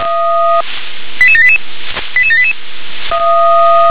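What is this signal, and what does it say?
Five-tone selective call (ZVEI), heard as quick runs of five tones, tones of about 70 ms. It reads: 88788 88788
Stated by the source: Received on an SDR, ZVEI five-tone code 88761.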